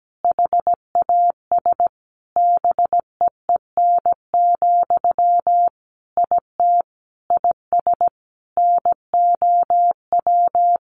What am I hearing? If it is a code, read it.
Morse code, heard as a keyed tone, about 17 wpm, 700 Hz. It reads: HAS BEEN, IT IS NOW
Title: Sentences from Top 1100 Words 17wpm